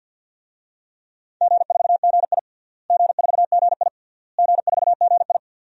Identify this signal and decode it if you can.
Morse code, heard as a keyed tone, about 46 wpm, 700 Hz. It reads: G4GI G4GI G4GI
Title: DXCC-100 Call Signs - 4 Characters - Repeated 3x - 50wpm